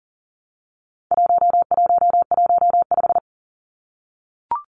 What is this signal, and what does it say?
Morse code, heard as a keyed tone, about 40 wpm, 700 Hz. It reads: 1115
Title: Numbers - 4-Digit 40wpm